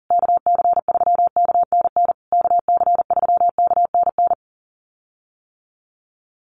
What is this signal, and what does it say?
Morse code, heard as a keyed tone, about 40 wpm, 700 Hz. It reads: KC3KNN KC3KNN
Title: USA Call Signs Repeated 3x 40wpm